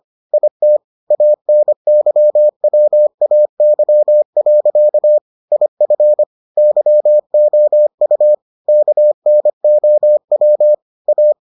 Morse code, heard as a keyed tone, about 25 wpm, 600 Hz. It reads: IT ANYWAY. IF YOU KNOW A